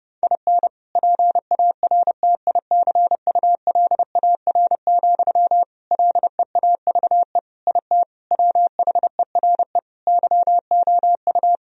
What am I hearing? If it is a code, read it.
Morse code, heard as a keyed tone, about 30 wpm, 700 Hz. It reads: IN PARTICULAR, LEAVE IT WHERE YOU
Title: Sentences from Top 1000 Words - Review Part 4 - 30wpm